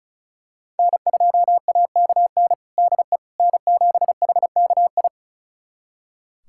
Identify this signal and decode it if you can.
Morse code, heard as a keyed tone, about 36 wpm, 700 Hz. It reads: N2AKN DE N7HKI
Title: USA Callsign DE Callsign - 5 Characters - 35wpm